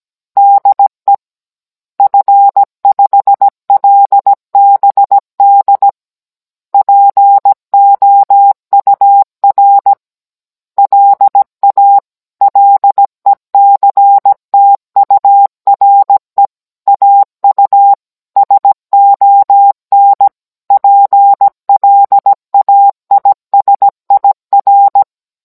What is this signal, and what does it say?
Morse code, heard as a keyed tone, about 17 wpm, 800 Hz. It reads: DE F5LBD POUR LA LECTURE AU SON PLAISIR